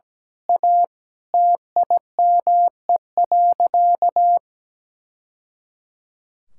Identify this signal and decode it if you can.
Morse code, heard as a keyed tone, about 17 wpm, 700 Hz. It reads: A TIME.
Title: Sentences from Top 1000 Words - Review Part 6 - 17wpm